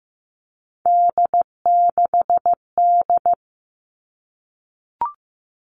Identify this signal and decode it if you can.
Morse code, heard as a keyed tone, about 15 wpm, 700 Hz. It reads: D6D
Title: All Cut Numbers - 3-Digit 15wpm